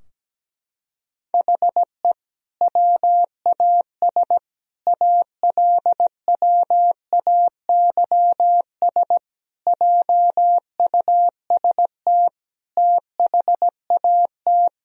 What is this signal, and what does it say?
Morse code, heard as a keyed tone, about 17 wpm, 700 Hz. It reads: HE WAS ALWAYS JUST THAT
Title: Sentences from Top 1000 Words - Review Part 4 - 17wpm